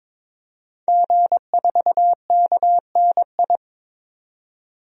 Morse code, heard as a keyed tone, about 22 wpm, 700 Hz. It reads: G4KNI
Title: DXCC-100 Call Signs - 5 Characters - 22wpm